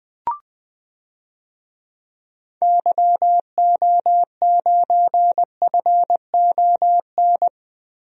Morse code, heard as a keyed tone, about 20 wpm, 700 Hz. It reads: YO9FON
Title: DXCC-100 Call Signs - 6 Characters - 20wpm